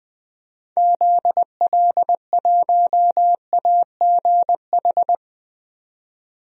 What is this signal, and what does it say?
Morse code, heard as a keyed tone, about 21 wpm, 700 Hz. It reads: ZL1AGH
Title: DXCC-100 Call Signs - 6 Characters - 20wpm